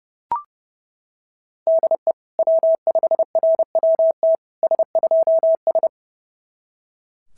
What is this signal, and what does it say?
Morse code, heard as a keyed tone, about 28 wpm, 660 Hz. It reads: DE W5RWT S2S
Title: SOTA QSO Practice 30wpm